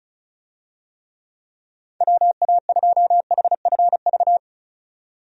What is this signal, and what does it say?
Morse code, heard as a keyed tone, about 34 wpm, 700 Hz. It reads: WA2HFV
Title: USA Call Signs - 6 Characters - 35wpm